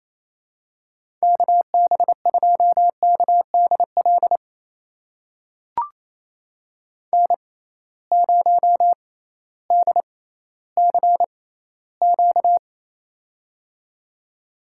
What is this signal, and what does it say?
Morse code, heard as a keyed tone, about 28 wpm, spaced as for 15 wpm, 700 Hz. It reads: KB2KDL N0DCQ
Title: USA Call Signs - 6x Character Spacing - 28wpm